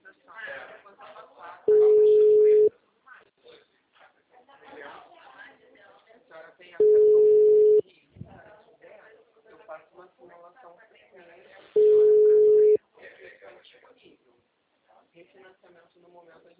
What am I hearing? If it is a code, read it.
Ringback tone (Europe, 425 Hz), with speech.
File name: out- -2016- - - .40570